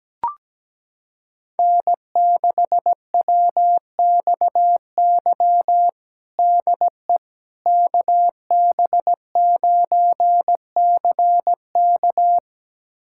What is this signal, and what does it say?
Morse code, heard as a keyed tone, about 17 wpm, 700 Hz. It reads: N6WXY DE KB9CK
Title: USA Callsign DE Callsign - 5 Characters - 17wpm